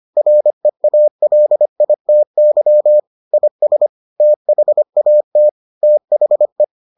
Morse code, heard as a keyed tone, about 25 wpm, 600 Hz. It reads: REALITY IS THAT THE